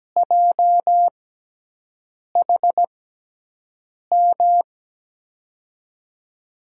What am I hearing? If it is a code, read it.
Morse code, heard as a keyed tone, about 17 wpm, spaced as for 6 wpm, 700 Hz. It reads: JHM
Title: USA Call Signs - 6 Characters - 6x Character Spacing - 17wpm